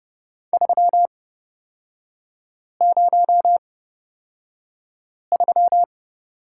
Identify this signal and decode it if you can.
Morse code, heard as a keyed tone, about 30 wpm, spaced as for 5 wpm, 690 Hz. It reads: 303
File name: Lesson 38e - Letter q - Three Characters 30wpm